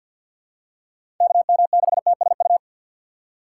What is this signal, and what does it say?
Morse code, heard as a keyed tone, about 50 wpm, 700 Hz. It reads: KM6TSU
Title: USA Call Signs - 6 Characters - 8x Character Spacing - 50wpm